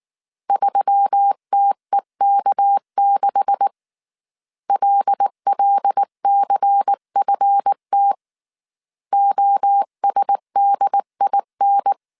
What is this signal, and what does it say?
Morse code, heard as a keyed tone, about 19 wpm, 800 Hz. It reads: 3TEK6 LLCFT OSDIN
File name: CW / DET / 1 9 1500C S B 18wpm 2551